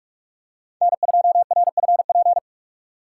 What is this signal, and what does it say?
Morse code, heard as a keyed tone, about 44 wpm, 700 Hz. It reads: N2RFP